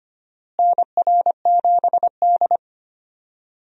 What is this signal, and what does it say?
Morse code, heard as a keyed tone, about 25 wpm, 700 Hz. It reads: NR7D
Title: USA Call Signs - 4 Characters - Repeated 3x - 25wpm